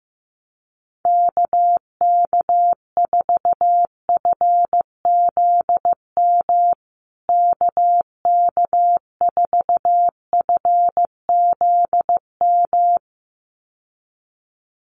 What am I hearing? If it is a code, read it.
Morse code, heard as a keyed tone, about 15 wpm, 700 Hz. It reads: KK4FZM KK4FZM
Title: USA Call Signs Repeated 2x 15wpm